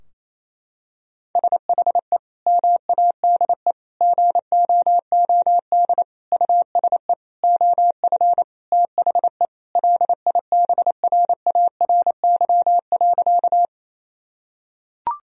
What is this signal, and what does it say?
Morse code, heard as a keyed tone, about 28 wpm, 700 Hz. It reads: SHE MADE GOOD USE OF THE LIBRARY.
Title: Sentences from Top 1000 Words - Review Part 4 - 28wpm